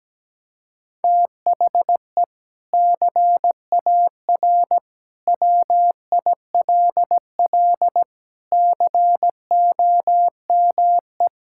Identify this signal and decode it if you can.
Morse code, heard as a keyed tone, about 17 wpm, 700 Hz. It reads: THE CAR WILL COME
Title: Sentences from Top 1600 Words 17wpm